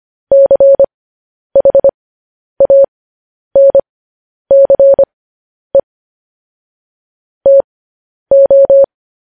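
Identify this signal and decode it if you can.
Morse code, heard as a keyed tone, about 25 wpm, spaced as for 10 wpm, 550 Hz. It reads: CHANCE TO
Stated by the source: Morse code news headlines at 10 wpm.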